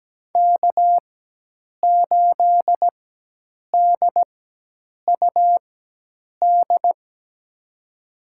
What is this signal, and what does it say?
Morse code, heard as a keyed tone, about 17 wpm, spaced as for 8 wpm, 700 Hz. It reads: K8DUD
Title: USA Call Signs - 5 Characters - 4x Character Spacing - 17wpm